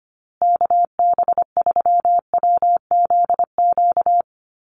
Morse code, heard as a keyed tone, about 25 wpm, 700 Hz. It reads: KB3WZQ
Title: USA Call Signs Repeated 3x 25wpm